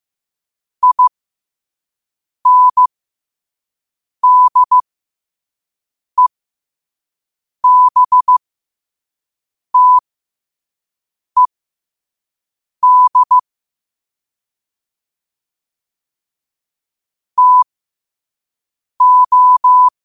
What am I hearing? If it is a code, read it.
Morse code, heard as a keyed tone, about 15 wpm, spaced as for 5 wpm, 1000 Hz. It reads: INDEBTED TO
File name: qotdfiles / 5wpm